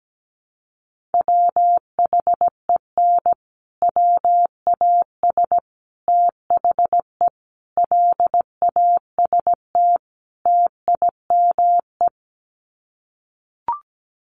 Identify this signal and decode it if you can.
Morse code, heard as a keyed tone, about 17 wpm, 700 Hz. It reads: WHEN WAS THE LAST TIME
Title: Sets of 5 Words Encore 17wpm